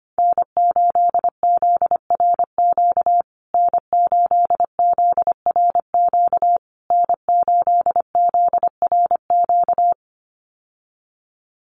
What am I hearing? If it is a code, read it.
Morse code, heard as a keyed tone, about 25 wpm, 700 Hz. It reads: N8ZRQ N8ZRQ N8ZRQ
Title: USA Call Signs Repeated 3x 25wpm